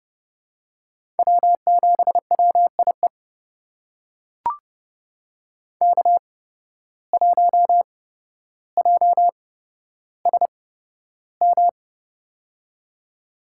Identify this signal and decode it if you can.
Morse code, heard as a keyed tone, about 31 wpm, spaced as for 13 wpm, 700 Hz. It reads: W7WIE K1JSM
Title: USA Call Signs - 5 Characters - 8x Character Spacing - 30wpm